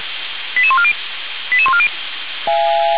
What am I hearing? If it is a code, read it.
Five-tone selective call (ZVEI), heard as quick runs of five tones, tones of about 70 ms. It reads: 88140 88140